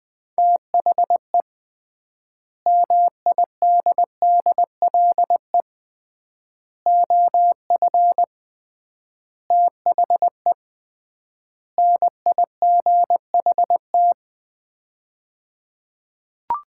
Morse code, heard as a keyed tone, about 21 wpm, 700 Hz. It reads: THE MIDDLE OF THE NIGHT
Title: 3x Word-Spacing - Sets of 5 Words - 20wpm